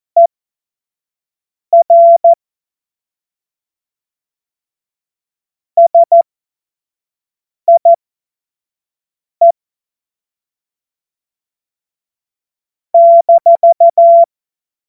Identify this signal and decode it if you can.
Morse code, heard as a keyed tone, about 14 wpm, spaced as for 5 wpm, 680 Hz. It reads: ER SIE -
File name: Morsekurs / Witze / Witz 11. Januar 2022, 25 F70